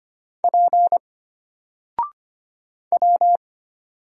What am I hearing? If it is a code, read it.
Morse code, heard as a keyed tone, about 25 wpm, 700 Hz. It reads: P W
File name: Single-Letter-Number-25wpm